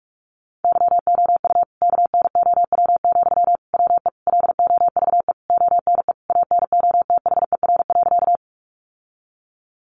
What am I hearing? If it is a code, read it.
Morse code, heard as a keyed tone, about 44 wpm, 700 Hz. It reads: YOU KNOW, WE LOVE ONE ANOTHER.
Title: Sentences from Top 600 Words 45wpm